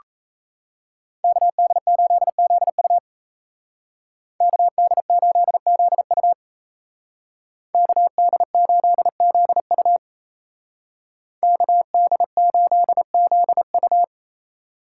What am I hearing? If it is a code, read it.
Morse code, heard as a keyed tone, about 36 wpm, 700 Hz. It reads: KD8ZU KD8ZU KD8ZU KD8ZU